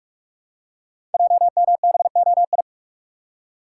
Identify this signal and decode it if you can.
Morse code, heard as a keyed tone, about 44 wpm, 700 Hz. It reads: JMBOI
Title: Code Groups - 5 Letters - 45wpm